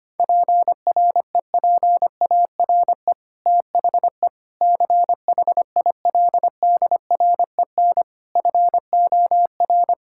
Morse code, heard as a keyed tone, about 25 wpm, 700 Hz. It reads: PREPARE THE CHILDREN FOR
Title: Sentences from Top 1000 Words - Review Part 6 - 25wpm